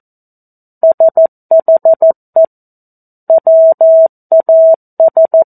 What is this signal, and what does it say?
Morse code, heard as a keyed tone, about 14 wpm, 650 Hz. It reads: SHE WAS
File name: 02-Chapter-02 14 WPM